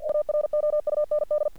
Morse code, heard as a keyed tone, about 48 wpm, 600 Hz. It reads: AROUND
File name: archived / CRSnet